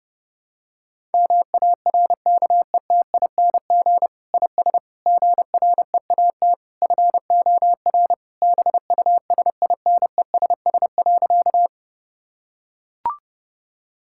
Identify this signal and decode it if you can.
Morse code, heard as a keyed tone, about 32 wpm, 700 Hz. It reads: MARKETING IS GREAT FOR BUSINESS.
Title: Sentences from Top 1000 Words - Review Part 5 - 30wpm